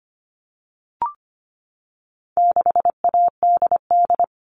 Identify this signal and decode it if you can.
Morse code, heard as a keyed tone, about 25 wpm, 700 Hz. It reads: 6ADD
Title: All Cut Numbers - 4-Digit 25wpm